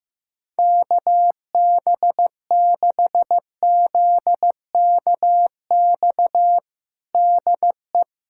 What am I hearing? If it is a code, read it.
Morse code, heard as a keyed tone, about 15 wpm, 700 Hz. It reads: KB6ZKX DE
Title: USA Callsign DE Callsign - 6 Characters - 15wpm